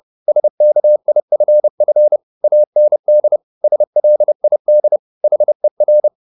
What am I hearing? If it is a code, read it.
Morse code, heard as a keyed tone, about 30 wpm, 600 Hz. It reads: SKIFF AND SLID HER
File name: the old man and the sea / the old man and the sea 1 30wpm